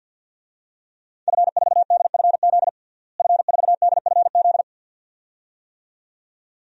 Morse code, heard as a keyed tone, about 52 wpm, 700 Hz. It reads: F4DFZ F4DFZ
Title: DXCC-100 Call Signs - 5 Characters - Repeated 2x - 50wpm